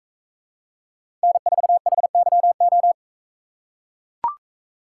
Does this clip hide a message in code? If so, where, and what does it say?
Yes: Morse code from 1.2 to 2.9 s, heard as a keyed tone, about 41 wpm, 700 Hz. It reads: N4HYO